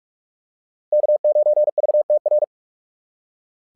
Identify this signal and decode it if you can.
Morse code, heard as a keyed tone, about 44 wpm, 600 Hz. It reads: K9VTR
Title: SOTA QSO Practice 45wpm